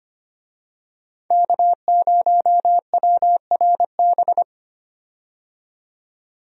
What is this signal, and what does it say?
Morse code, heard as a keyed tone, about 25 wpm, 700 Hz. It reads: K0WRB
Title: USA Call Signs - 5 Characters - 25wpm